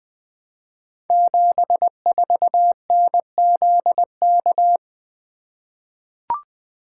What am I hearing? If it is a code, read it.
Morse code, heard as a keyed tone, about 20 wpm, 700 Hz. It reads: 74NZK
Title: Code Groups - 5 Letter:Numbers - 20wpm